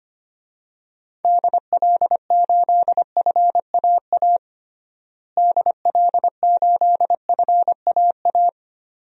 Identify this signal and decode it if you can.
Morse code, heard as a keyed tone, about 25 wpm, 700 Hz. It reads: DL8FAA DL8FAA
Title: DXCC-100 Call Signs - 6 Characters - Repeated 2x - 25wpm